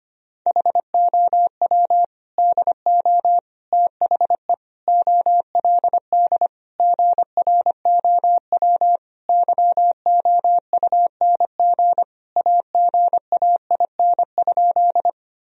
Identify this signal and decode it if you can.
Morse code, heard as a keyed tone, about 25 wpm, 700 Hz. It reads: HOW DO THE OLD GROW YOUNG AGAIN?
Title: Sentences from Top 1000 Words - Review Part 1 - 25wpm